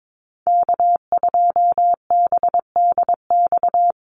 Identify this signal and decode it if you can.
Morse code, heard as a keyed tone, about 22 wpm, 700 Hz. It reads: K2BDX
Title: USA Call Signs Repeated 2x 22wpm